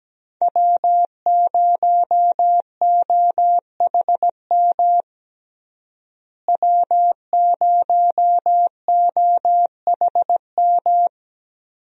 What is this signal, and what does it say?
Morse code, heard as a keyed tone, about 17 wpm, 700 Hz. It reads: W0OHM W0OHM
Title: USA Call Signs - 5 Characters - Repeated 2x - 17wpm